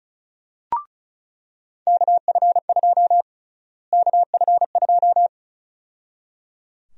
Morse code, heard as a keyed tone, about 34 wpm, 700 Hz. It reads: KF2 KF2
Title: USA Call Sign Prefixes - Repeated 2x - 35wpm